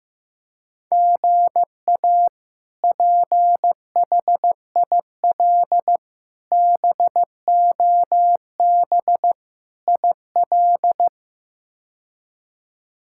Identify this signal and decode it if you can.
Morse code, heard as a keyed tone, about 16 wpm, 700 Hz. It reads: GA PHIL BOB IL
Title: K1USN SST QSO Practice - 15wpm